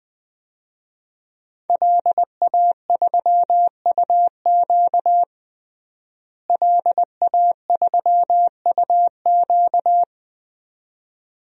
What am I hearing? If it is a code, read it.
Morse code, heard as a keyed tone, about 20 wpm, 700 Hz. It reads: LA3UQ LA3UQ
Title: DXCC-100 Call Signs - 5 Characters - Repeated 2x - 20wpm